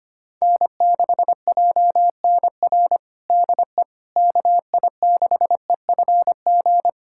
Morse code, heard as a keyed tone, about 25 wpm, 700 Hz. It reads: N6JNR DE KI6EFG
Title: USA Callsign DE Callsign 25wpm